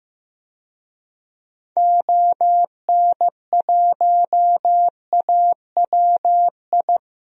Morse code, heard as a keyed tone, about 15 wpm, 700 Hz. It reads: ON1AWI
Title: DXCC-100 Call Signs - 6 Characters - 15wpm